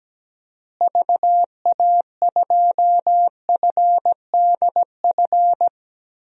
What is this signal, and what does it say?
Morse code, heard as a keyed tone, about 17 wpm, 700 Hz. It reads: VA2FDF
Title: DXCC-100 Call Signs - 6 Characters - Repeated 2x - 17wpm